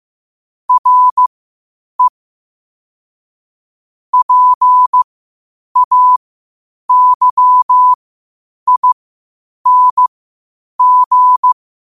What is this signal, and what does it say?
Morse code, heard as a keyed tone, about 15 wpm, spaced as for 8 wpm, 1000 Hz. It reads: RE PAYING
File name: qotdfiles / 8wpm